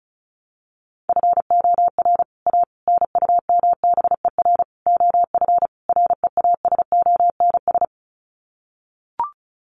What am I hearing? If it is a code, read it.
Morse code, heard as a keyed tone, about 34 wpm, 700 Hz. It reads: FOR A NUMBER OF REASONS
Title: Sets of 5 Words Encore2 35wpm